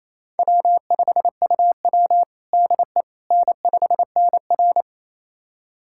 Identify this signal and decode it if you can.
Morse code, heard as a keyed tone, about 28 wpm, 700 Hz. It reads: W5UW DE N5NR
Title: USA Callsign DE Callsign - 4 Characters - 28wpm